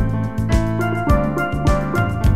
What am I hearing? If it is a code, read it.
There is no code.